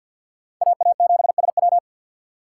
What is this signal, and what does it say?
Morse code, heard as a keyed tone, about 50 wpm, 700 Hz. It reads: AA7SW
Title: USA Call Signs - 6x Character Spacing - 50wpm